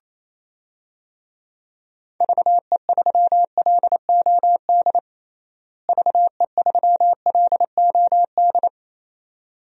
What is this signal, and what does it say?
Morse code, heard as a keyed tone, about 28 wpm, 700 Hz. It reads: VE3LOD VE3LOD
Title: DXCC-100 Call Signs - 6 Characters - Repeated 2x - 28wpm